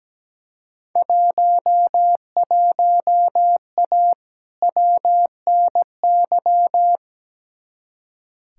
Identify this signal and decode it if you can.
Morse code, heard as a keyed tone, about 17 wpm, 700 Hz. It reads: 11A WNY
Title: Field Day Exchanges Encore - 17wpm